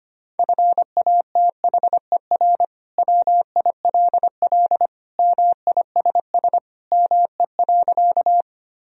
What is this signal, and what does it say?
Morse code, heard as a keyed tone, about 25 wpm, 700 Hz. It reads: FATHER WILL MISS ME.
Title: Sentences from Top 1000 Words - Review Part 3 - 25wpm